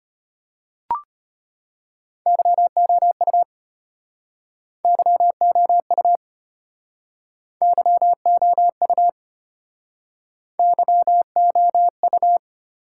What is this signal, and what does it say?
Morse code, heard as a keyed tone, about 33 wpm, 700 Hz. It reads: YOU YOU YOU YOU